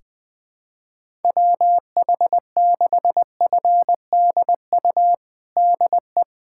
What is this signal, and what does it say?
Morse code, heard as a keyed tone, about 20 wpm, 700 Hz. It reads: WH6FDU DE
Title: USA Callsign DE Callsign - 6 Characters - 20wpm